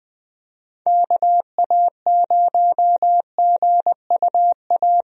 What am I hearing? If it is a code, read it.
Morse code, heard as a keyed tone, about 21 wpm, 700 Hz. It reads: KA0GUA
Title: USA Call Signs - 6 Characters - 20wpm